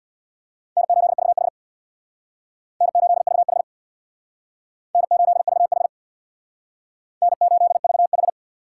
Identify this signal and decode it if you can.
Morse code, heard as a keyed tone, about 62 wpm, 700 Hz. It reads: N8VH N8VH N8VH N8VH
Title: Speed Racing - USA Call Signs 50wpm